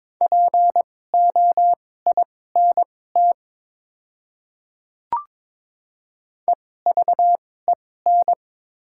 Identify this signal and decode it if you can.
Morse code, heard as a keyed tone, about 22 wpm, spaced as for 16 wpm, 700 Hz. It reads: POINT EVEN
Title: 2x Character Spacing - Top 200 Words - 22wpm